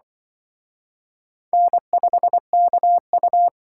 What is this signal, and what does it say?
Morse code, heard as a keyed tone, about 24 wpm, 700 Hz. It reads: N5KU
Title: Speed Racing - USA Call Signs 20wpm